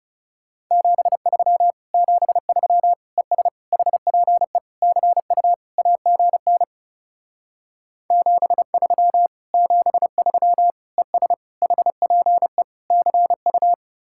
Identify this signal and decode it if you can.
Morse code, heard as a keyed tone, about 36 wpm, 700 Hz. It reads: 73 73 ES HPE CU AGN 73 73 ES HPE CU
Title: Speed-Racing Common QSO Phrases - 30wpm